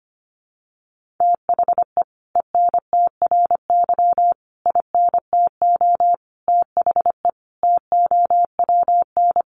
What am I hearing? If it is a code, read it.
Morse code, heard as a keyed tone, about 25 wpm, 700 Hz. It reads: THE ENTRY INTO THE TOWN
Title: Sentences from Top 1500 Words 25wpm